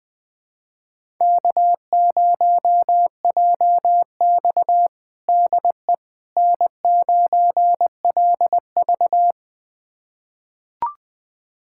Morse code, heard as a keyed tone, about 20 wpm, 700 Hz. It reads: K0JX DE N9LV
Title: USA Callsign DE Callsign - 4 Characters - 20wpm